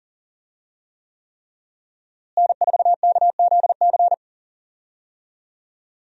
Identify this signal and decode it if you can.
Morse code, heard as a keyed tone, about 40 wpm, 700 Hz. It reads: N4KZC
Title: USA Call Signs - 5 Characters - Repeated 3x - 40wpm